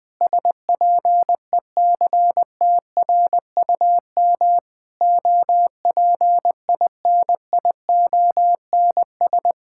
Morse code, heard as a keyed tone, about 21 wpm, 700 Hz. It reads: SPECTRUM OPINIONS
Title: Sentences with 7-Word Phrases - Spoken Beginning - 20wpm